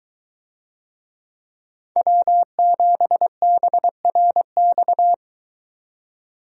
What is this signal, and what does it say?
Morse code, heard as a keyed tone, about 23 wpm, 700 Hz. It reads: W7BRX